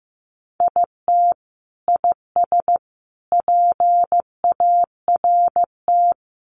Morse code, heard as a keyed tone, about 15 wpm, 700 Hz. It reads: IT IS PART
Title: Sentences from Top 1100 Words 15wpm